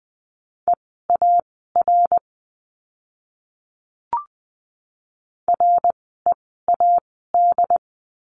Morse code, heard as a keyed tone, about 20 wpm, spaced as for 15 wpm, 700 Hz. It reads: EAR READ
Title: 068 Lesson 12f - Letter d - Words 20wpm